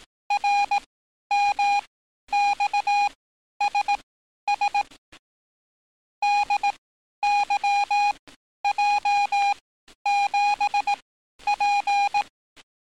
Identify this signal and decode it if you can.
Morse code, heard as a keyed tone, about 18 wpm, spaced as for 11 wpm, 820 Hz. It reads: RMXSS DYJ7P
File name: training / NAVPERS 11049to11060 / NAVPERS 11050 RA RB